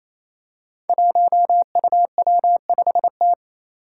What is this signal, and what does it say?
Morse code, heard as a keyed tone, about 28 wpm, 700 Hz. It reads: 1UW5T
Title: Five Challenging Letter-Number Combinations 28wpm